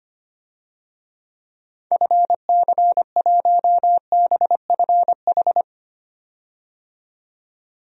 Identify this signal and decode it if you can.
Morse code, heard as a keyed tone, about 25 wpm, 700 Hz. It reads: FC1BFH
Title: DXCC-100 Call Signs - 6 Characters - 25wpm